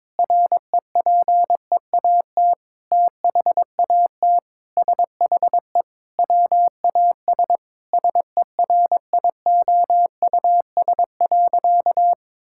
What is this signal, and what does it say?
Morse code, heard as a keyed tone, about 22 wpm, 700 Hz. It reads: REPEAT THAT SHE WAS SERIOUS.
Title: Sentences from Top 1600 Words 22wpm